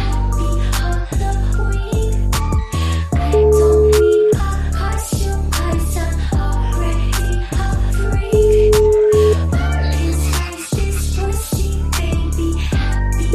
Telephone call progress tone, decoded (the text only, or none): ringback tone